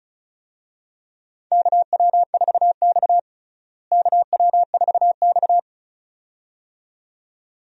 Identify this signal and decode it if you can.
Morse code, heard as a keyed tone, about 34 wpm, 700 Hz. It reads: KW4X KW4X
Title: USA Call Signs - 4 Characters - Repeated 2x - 35wpm